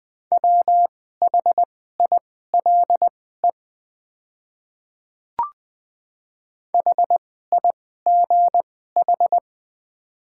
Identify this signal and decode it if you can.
Morse code, heard as a keyed tone, about 21 wpm, spaced as for 15 wpm, 700 Hz. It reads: WHILE HIGH
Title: 2x Character Spacing - Top 300 Words - 20wpm